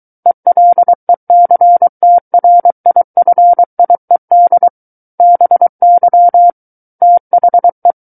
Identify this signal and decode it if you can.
Morse code, heard as a keyed tone, about 23 wpm, 700 Hz. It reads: ELECTRIFIED BY THE
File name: Book One / 14-Chapter 14 - 23 WPM